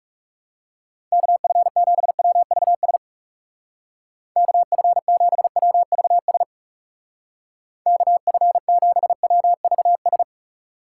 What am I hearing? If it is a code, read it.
Morse code, heard as a keyed tone, about 36 wpm, 700 Hz. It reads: KF7WVS KF7WVS KF7WVS
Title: Speed Racing - USA 2x3 Call Signs 30wpm